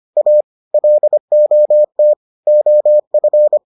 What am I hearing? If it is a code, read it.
Morse code, heard as a keyed tone, about 25 wpm, 600 Hz. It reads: A LOT OF